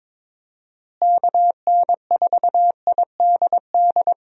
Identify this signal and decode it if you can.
Morse code, heard as a keyed tone, about 21 wpm, 700 Hz. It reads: KN4IDD
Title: USA Call Signs - 6 Characters - Repeated 3x - 22wpm